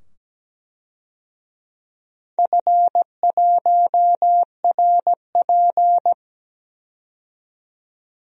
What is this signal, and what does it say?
Morse code, heard as a keyed tone, about 17 wpm, 700 Hz. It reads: F1RP
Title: DXCC-100 Call Signs - 4 Characters - 17wpm